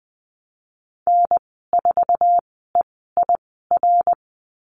Morse code, heard as a keyed tone, about 20 wpm, spaced as for 15 wpm, 700 Hz. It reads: N4EIR
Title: 076 Lesson 13g - Letter L - Call Signs 20wpm